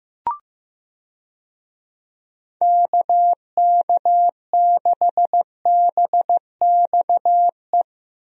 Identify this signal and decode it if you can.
Morse code, heard as a keyed tone, about 15 wpm, 700 Hz. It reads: KK6BXE
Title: USA Call Signs - 6 Characters - Repeated 3x - 15wpm